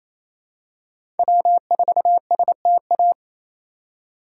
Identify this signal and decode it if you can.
Morse code, heard as a keyed tone, about 28 wpm, 700 Hz. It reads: W4STA
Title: USA Call Signs - 3x Character Spacing - 28wpm